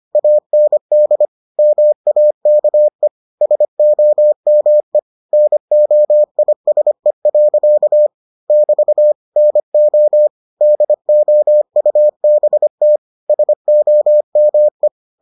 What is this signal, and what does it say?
Morse code, heard as a keyed tone, about 25 wpm, 600 Hz. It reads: AND MAKE SOME NOISE. <BT> NO DOUBT SOME